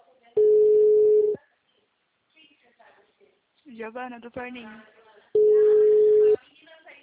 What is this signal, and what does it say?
Ringback tone (Europe, 425 Hz), with speech.
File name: out- -2014- - - .24836